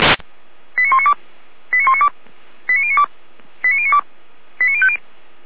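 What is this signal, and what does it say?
Five-tone selective call (ZVEI), heard as quick runs of five tones, tones of about 80 ms. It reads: 89192 89192 89092 89092 89050
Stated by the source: Five-tone ZVEI selective call received on an SDR.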